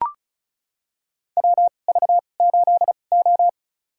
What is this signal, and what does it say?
Morse code, heard as a keyed tone, about 36 wpm, spaced as for 25 wpm, 700 Hz. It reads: WV8O
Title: USA Call Signs - 4 Characters - 2x Character Spacing - 35wpm